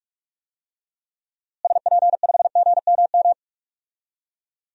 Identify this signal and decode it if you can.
Morse code, heard as a keyed tone, about 43 wpm, 700 Hz. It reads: SP5GMM
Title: DXCC-100 Call Signs - 6 Characters - Repeated 2x - 45wpm